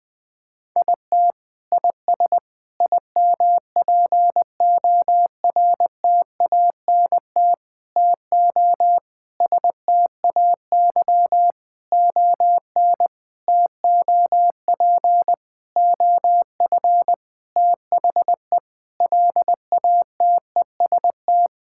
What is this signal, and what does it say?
Morse code, heard as a keyed tone, about 20 wpm, 700 Hz. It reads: IT IS IMPORTANT TO STAY ON TOP OF THE LATEST
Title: Sentences with 6-Word Phrases - Spoken Beginning - 20wpm